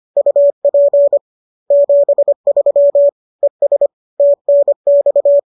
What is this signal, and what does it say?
Morse code, heard as a keyed tone, about 25 wpm, 570 Hz. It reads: UP 73 ES TNX